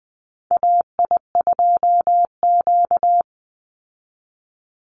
Morse code, heard as a keyed tone, about 20 wpm, 700 Hz. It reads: AI2Q
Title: Call Signs Repeated 3x - 20wpm